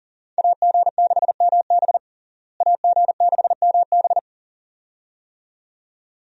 Morse code, heard as a keyed tone, about 40 wpm, 700 Hz. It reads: AG6MB AG6MB
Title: USA Call Signs - 5 Characters - Repeated 2x - 40wpm